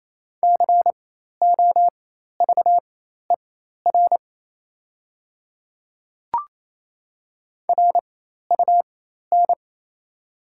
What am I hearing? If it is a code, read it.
Morse code, heard as a keyed tone, about 28 wpm, spaced as for 13 wpm, 690 Hz. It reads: COVER RUN
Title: 4x Character Spacing - Top 300 Words - 28wpm